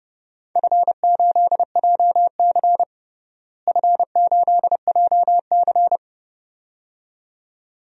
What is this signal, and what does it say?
Morse code, heard as a keyed tone, about 30 wpm, 700 Hz. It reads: F8JC F8JC
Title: DXCC-100 Call Signs - 4 Characters - Repeated 2x - 30wpm